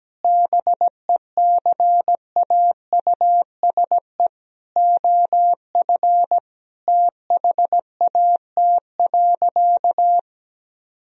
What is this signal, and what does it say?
Morse code, heard as a keyed tone, about 17 wpm, 700 Hz. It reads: BECAUSE OF THAT.
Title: Sentences from Top 1000 Words - Review Part 3 - 17wpm